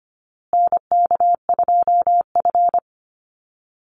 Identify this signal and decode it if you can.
Morse code, heard as a keyed tone, about 24 wpm, 700 Hz. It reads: NK2F
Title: Call Signs Repeated 3x - 25wpm